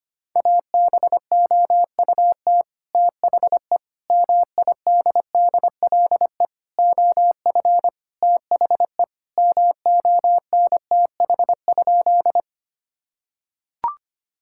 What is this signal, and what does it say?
Morse code, heard as a keyed tone, about 25 wpm, 700 Hz. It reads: ABOUT THE MIDDLE OF THE MONTH?
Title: Sentences from Top 1000 Words - Review Part 6 - 25wpm